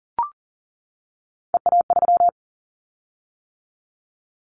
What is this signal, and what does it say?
Morse code, heard as a keyed tone, about 40 wpm, 700 Hz. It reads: EA3
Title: Callsign Prefixes for Top 25 DXCC - 40wpm